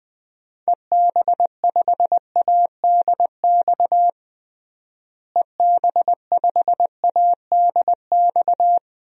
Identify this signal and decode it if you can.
Morse code, heard as a keyed tone, about 20 wpm, 700 Hz. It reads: EB5ADX EB5ADX
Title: DXCC-100 Call Signs - 6 Characters - Repeated 2x - 20wpm